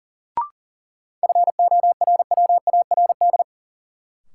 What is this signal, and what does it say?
Morse code, heard as a keyed tone, about 40 wpm, 700 Hz. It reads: FORWARD
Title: Sentences from Top 1300 Words - New Word Drill Alt - 40wpm